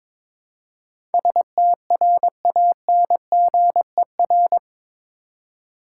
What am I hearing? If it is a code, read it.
Morse code, heard as a keyed tone, about 23 wpm, 700 Hz. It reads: STRANGER
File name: practice / 22wpm / Eight Letter Words - 22wpm